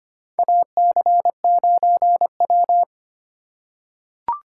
Morse code, heard as a keyed tone, about 25 wpm, 700 Hz. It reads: AC9W
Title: USA Call Signs - 4 Characters - 3x Character Spacing - 25wpm